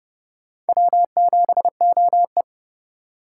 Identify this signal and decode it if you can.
Morse code, heard as a keyed tone, about 27 wpm, 700 Hz. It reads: W7OE